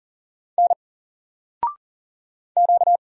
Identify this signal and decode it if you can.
Morse code, heard as a keyed tone, about 40 wpm, 700 Hz. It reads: N Q
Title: Single Letter-Number 40wpm